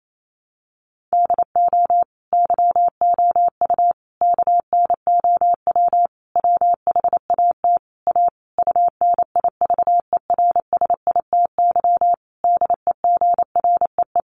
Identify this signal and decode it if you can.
Morse code, heard as a keyed tone, about 28 wpm, 700 Hz. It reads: DO YOU KNOW WHAT A UNIVERSITY DEGREE
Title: Sentences from Top 1500 Words 28wpm